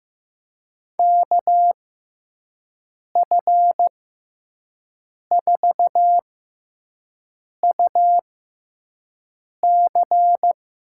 Morse code, heard as a keyed tone, about 15 wpm, spaced as for 5 wpm, 700 Hz. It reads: KF4UC
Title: USA Call Signs - 6 Characters - 6x Character Spacing - 15wpm